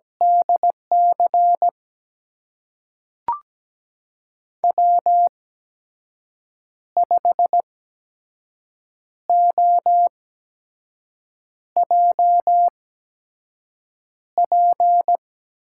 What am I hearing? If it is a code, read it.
Morse code, heard as a keyed tone, about 17 wpm, spaced as for 5 wpm, 700 Hz. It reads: DC W5OJP
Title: USA Call Signs - 8x Character Spacing - 17wpm